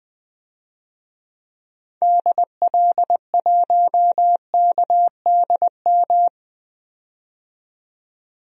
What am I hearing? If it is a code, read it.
Morse code, heard as a keyed tone, about 20 wpm, 700 Hz. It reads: DL1KDM
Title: DXCC-100 Call Signs - 6 Characters - 20wpm